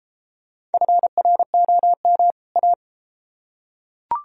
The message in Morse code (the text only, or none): FROM A